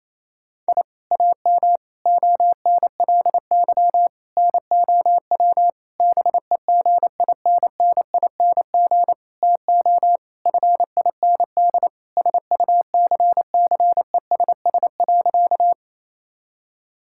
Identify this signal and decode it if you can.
Morse code, heard as a keyed tone, about 28 wpm, 700 Hz. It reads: I AM ONLY NOW BEGINNING TO FIND SUCCESS.
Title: Sentences from Top 1000 Words - Review Part 5 - 28wpm